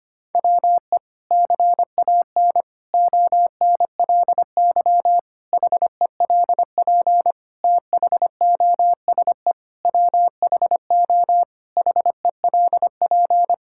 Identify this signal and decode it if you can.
Morse code, heard as a keyed tone, about 25 wpm, 700 Hz. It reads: WE CAN ONLY HELP THOSE WHO HELP
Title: Sentences from Top 1000 Words - Review Part 5 - 25wpm